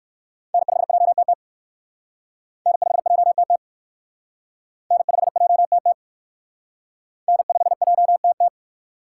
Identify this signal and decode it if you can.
Morse code, heard as a keyed tone, about 63 wpm, 700 Hz. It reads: N5JTT N5JTT N5JTT N5JTT